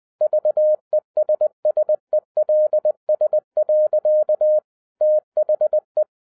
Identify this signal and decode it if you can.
Morse code, heard as a keyed tone, about 20 wpm, 610 Hz. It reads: VESSELS. THE